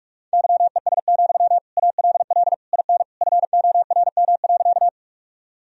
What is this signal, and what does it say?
Morse code, heard as a keyed tone, about 46 wpm, 700 Hz. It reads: YES, ALL IN FORM.
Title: Sentences from Top 1000 Words - Review Part 1 - 45wpm